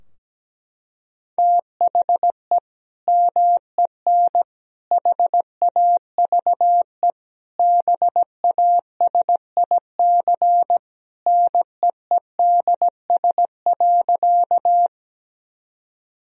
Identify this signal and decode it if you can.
Morse code, heard as a keyed tone, about 17 wpm, 700 Hz. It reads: THE MEN HAVE BASIC NEEDS.
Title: Sentences from Top 1000 Words - Review Part 3 - 17wpm